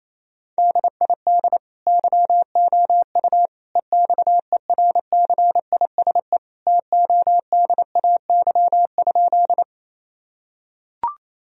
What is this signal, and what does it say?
Morse code, heard as a keyed tone, about 28 wpm, 700 Hz. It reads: DID YOU EXERCISE TODAY?
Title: Sentences from Top 1000 Words - Review Part 4 - 28wpm